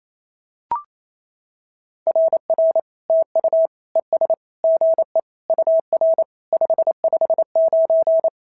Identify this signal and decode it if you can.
Morse code, heard as a keyed tone, about 25 wpm, 650 Hz. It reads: RR TU ES GE UR 559
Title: SOTA QSO Practice 28wpm